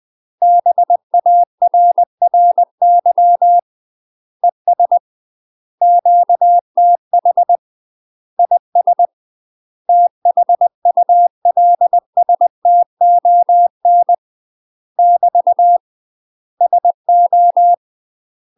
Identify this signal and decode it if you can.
Morse code, heard as a keyed tone, about 20 wpm, 700 Hz. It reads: BARRY ES QTH IS THULSTON <BT> SO